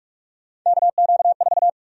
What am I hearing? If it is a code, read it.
Morse code, heard as a keyed tone, about 45 wpm, 700 Hz. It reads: KQ4
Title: USA Call Sign Prefixes - 45wpm